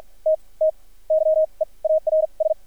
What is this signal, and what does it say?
Morse code, heard as a keyed tone, about 50 wpm, spaced as for 30 wpm, 630 Hz. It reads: T TYEARS
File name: archived / CRSnet